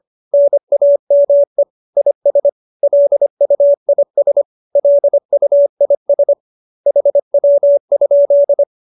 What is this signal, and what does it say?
Morse code, heard as a keyed tone, about 25 wpm, 570 Hz. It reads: NAME IS LUIS LUIS HW?
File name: QSO304 25